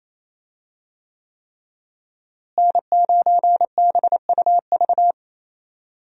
Morse code, heard as a keyed tone, about 28 wpm, 700 Hz. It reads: N9BUV
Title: USA Call Signs - 5 Characters - Repeated 3x - 28wpm